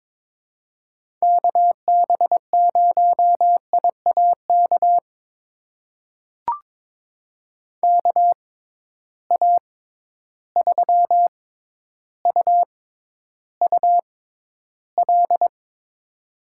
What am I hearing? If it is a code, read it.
Morse code, heard as a keyed tone, about 22 wpm, spaced as for 11 wpm, 700 Hz. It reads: KB0IAK KA3UUL
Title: USA Call Signs - 6x Character Spacing - 22wpm